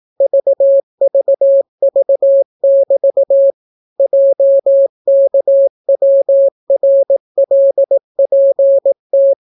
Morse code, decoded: VVV<BT> JKWRLPT

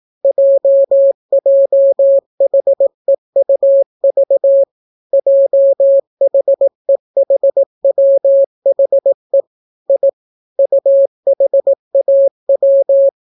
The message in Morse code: JJHEUV JHEHWHE I UHAW

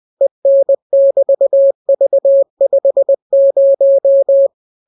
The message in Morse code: EN<BT>V50

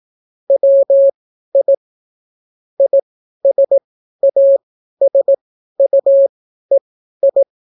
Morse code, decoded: WI ISASUEI

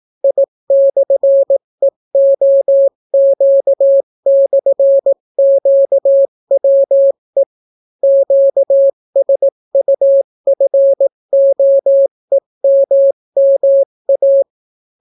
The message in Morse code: I/EOQ/QWE QSUFOEMMA

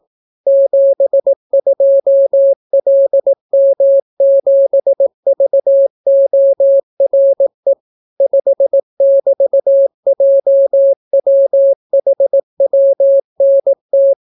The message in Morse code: 72LM7VORE 5<BT>JWHWNT

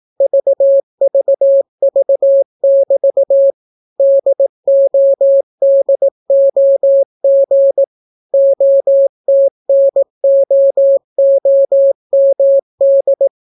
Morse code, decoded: VVV<BT> DODOG OTNOOMD

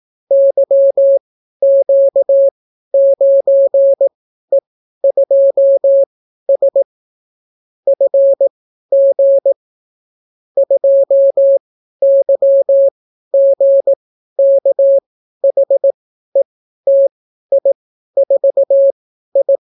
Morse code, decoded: YQ9E2S FG 2YGKHETI4I

